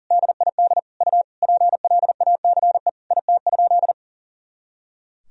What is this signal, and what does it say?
Morse code, heard as a keyed tone, about 40 wpm, 700 Hz. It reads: DID U PLACE IT?
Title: Sentences from Top 200 Words and Abbreviations 40wpm